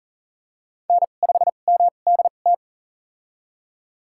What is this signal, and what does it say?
Morse code, heard as a keyed tone, about 40 wpm, spaced as for 29 wpm, 700 Hz. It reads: N5MDT
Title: USA Call Signs - 2x Character Spacing - 40wpm